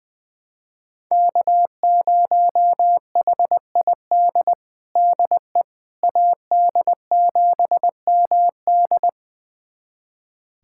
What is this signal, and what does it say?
Morse code, heard as a keyed tone, about 20 wpm, 700 Hz. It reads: K0HID DE AD7MD
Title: USA Callsign DE Callsign - 5 Characters - 20wpm